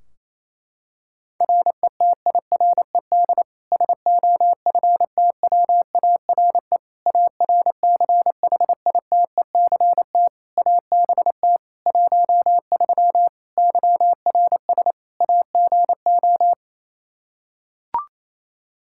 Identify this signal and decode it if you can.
Morse code, heard as a keyed tone, about 28 wpm, 700 Hz. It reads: RETIRED SOFTWARE ARCHITECT ABT 13 YRS AGO
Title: Practice QSO - Exchange 3 - More Info - 28wpm